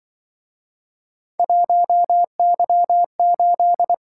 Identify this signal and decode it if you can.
Morse code, heard as a keyed tone, about 25 wpm, 700 Hz. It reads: 1Y8